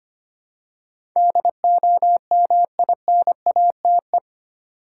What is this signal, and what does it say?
Morse code, heard as a keyed tone, about 25 wpm, 700 Hz. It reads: DOMINATE
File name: Eight Letter Words - 25wpm